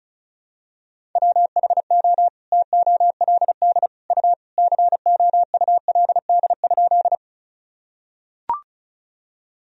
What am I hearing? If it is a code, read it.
Morse code, heard as a keyed tone, about 36 wpm, 700 Hz. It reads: WHO TOLD U COULD?
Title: Sentences from Top 400 Words and Abbreviations 35wpm